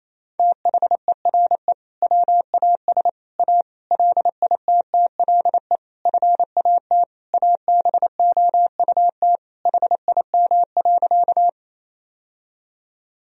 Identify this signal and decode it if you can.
Morse code, heard as a keyed tone, about 28 wpm, 700 Hz. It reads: THERE WAS A LITTLE FAT ABOUT HIM.
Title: Sentences from Top 1000 Words - Review Part 4 - 28wpm